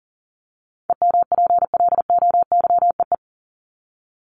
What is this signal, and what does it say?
Morse code, heard as a keyed tone, about 40 wpm, 700 Hz. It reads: EMPLOYEE
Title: Top 1501-1750 Words 40wpm